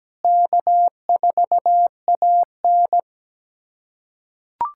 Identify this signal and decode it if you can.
Morse code, heard as a keyed tone, about 17 wpm, 700 Hz. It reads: K4AN
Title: USA Callsign DE Callsign - 4 Characters - 17wpm